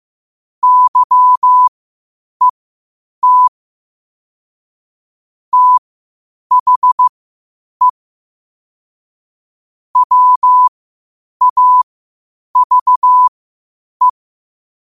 Morse code, heard as a keyed tone, about 15 wpm, spaced as for 8 wpm, 1000 Hz. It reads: YET THE WAVE
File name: qotdfiles / 8wpm